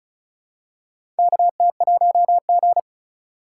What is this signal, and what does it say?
Morse code, heard as a keyed tone, about 36 wpm, 700 Hz. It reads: KT1G
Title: USA Call Signs - 4 Characters - 3x Character Spacing - 35wpm